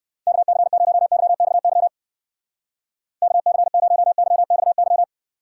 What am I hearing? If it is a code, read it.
Morse code, heard as a keyed tone, about 56 wpm, 700 Hz. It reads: KC0YXY KC0YXY